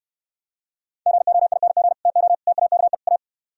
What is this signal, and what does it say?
Morse code, heard as a keyed tone, about 68 wpm, 700 Hz. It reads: GOING TO TAKE A